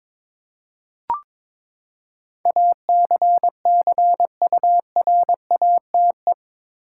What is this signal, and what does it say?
Morse code, heard as a keyed tone, about 22 wpm, 700 Hz. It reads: ACCURATE
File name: practice / 22wpm / Eight Letter Words - 22wpm